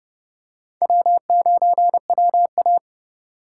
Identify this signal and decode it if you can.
Morse code, heard as a keyed tone, about 30 wpm, 700 Hz. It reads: W9WA